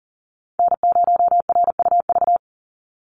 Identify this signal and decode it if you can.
Morse code, heard as a keyed tone, about 40 wpm, 700 Hz. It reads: N0RUV
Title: USA Call Signs 40wpm